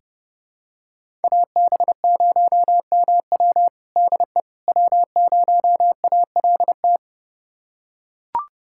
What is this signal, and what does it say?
Morse code, heard as a keyed tone, about 32 wpm, 700 Hz. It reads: AB0MW DE W0ALT
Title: USA Callsign DE Callsign - 5 Characters - 30wpm